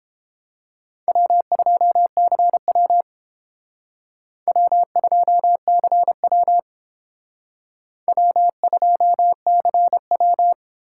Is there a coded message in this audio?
Yes: Morse code from 1.1 to 10.5 s, heard as a keyed tone, about 30 wpm, 700 Hz. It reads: W2CW W2CW W2CW